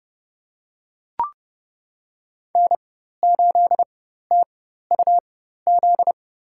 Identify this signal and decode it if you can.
Morse code, heard as a keyed tone, about 30 wpm, spaced as for 14 wpm, 700 Hz. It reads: N8TUZ
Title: USA Call Signs - 5 Characters - 4x Character Spacing - 30wpm